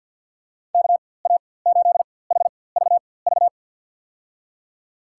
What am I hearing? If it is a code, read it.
Morse code, heard as a keyed tone, about 52 wpm, spaced as for 23 wpm, 700 Hz. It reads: KA8HVV
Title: USA Call Signs - 6 Characters - 4x Character Spacing - 50wpm